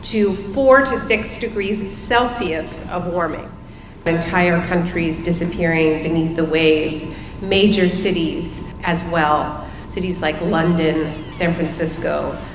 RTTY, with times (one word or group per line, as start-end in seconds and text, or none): none